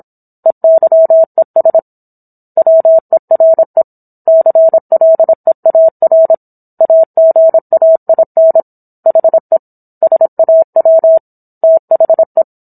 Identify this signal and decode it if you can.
Morse code, heard as a keyed tone, about 27 wpm, 650 Hz. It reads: EYES WERE CLEAR AGAIN HE SAW THE